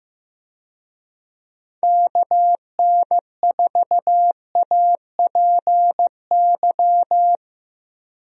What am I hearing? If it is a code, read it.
Morse code, heard as a keyed tone, about 15 wpm, 700 Hz. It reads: KN4APY
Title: USA Call Signs - 6 Characters - 15wpm